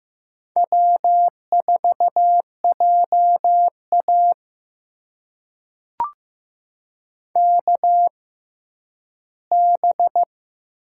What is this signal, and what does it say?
Morse code, heard as a keyed tone, about 15 wpm, spaced as for 10 wpm, 700 Hz. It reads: W4JA KB
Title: USA Call Signs - 6x Character Spacing - 15wpm